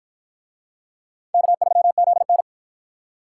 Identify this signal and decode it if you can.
Morse code, heard as a keyed tone, about 48 wpm, 700 Hz. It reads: K3ZN